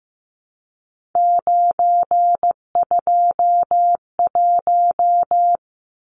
Morse code, heard as a keyed tone, about 15 wpm, 700 Hz. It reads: 921
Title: Numbers - 3-Digit 15wpm